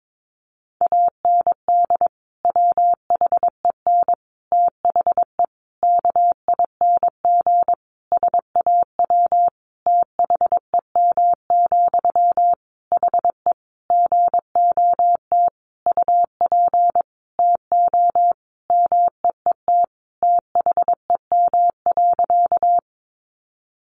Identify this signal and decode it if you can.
Morse code, heard as a keyed tone, about 22 wpm, 700 Hz. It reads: AND WHEN THE KING SAW THEM, HE GOT UP TO MEET THEM.
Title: Sentences from Top 1100 Words 22wpm